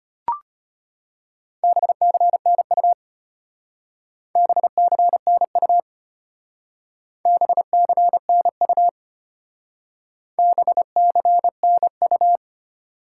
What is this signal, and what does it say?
Morse code, heard as a keyed tone, about 32 wpm, 700 Hz. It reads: BCNU BCNU BCNU BCNU